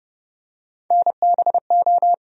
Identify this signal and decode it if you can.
Morse code, heard as a keyed tone, about 30 wpm, 700 Hz. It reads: NBO
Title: USA Call Sign Postfixes - 3 Letters 30wpm